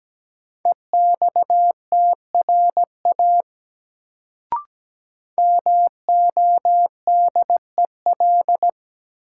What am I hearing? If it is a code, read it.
Morse code, heard as a keyed tone, about 17 wpm, 700 Hz. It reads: EXTRA MODEL
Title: Sentences from Top 1000 Words - New Word Drill - 17wpm